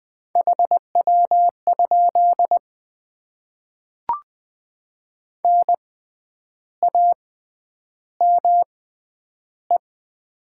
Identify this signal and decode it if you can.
Morse code, heard as a keyed tone, about 20 wpm, spaced as for 9 wpm, 700 Hz. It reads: HW? NAME
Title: Top 20 QSO Elements - 6x Character Spacing - 20wpm